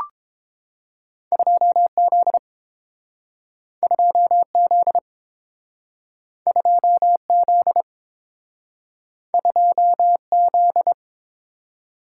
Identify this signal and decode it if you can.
Morse code, heard as a keyed tone, about 30 wpm, 700 Hz. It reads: 2Z 2Z 2Z 2Z